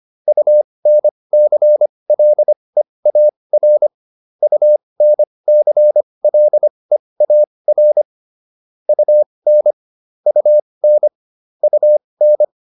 Morse code, heard as a keyed tone, about 25 wpm, spaced as for 20 wpm, 600 Hz. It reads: UNCLEAR UNCLEAR UN UN UN